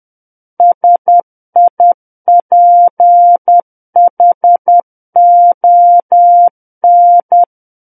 Morse code, heard as a keyed tone, about 10 wpm, 700 Hz. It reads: SIPHON
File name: Book One / 01-Chapter 1 - 10 WPM